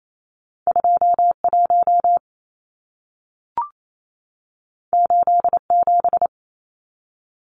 Morse code, heard as a keyed tone, about 28 wpm, 700 Hz. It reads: 21 87
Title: Numbers - 2-Digit 28wpm